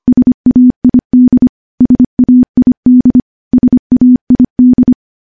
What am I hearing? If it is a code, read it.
Morse code, heard as a keyed tone, about 25 wpm, 250 Hz. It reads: SAID SAID SAID
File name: morse / morse-training-25WPM